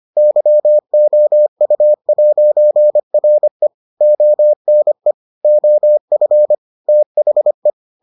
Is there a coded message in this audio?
Yes: Morse code from 0.2 to 7.7 s, heard as a keyed tone, about 25 wpm, 600 Hz. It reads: YOU'RE ONE OF THE